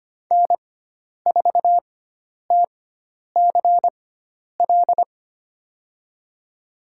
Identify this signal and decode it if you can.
Morse code, heard as a keyed tone, about 25 wpm, spaced as for 10 wpm, 700 Hz. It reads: N4TCL
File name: Lesson 16g - Letter c - Call Signs 25wpm